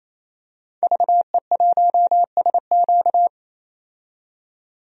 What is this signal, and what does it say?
Morse code, heard as a keyed tone, about 28 wpm, 700 Hz. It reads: VE1SQ
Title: DXCC-100 Call Signs - 5 Characters - 28wpm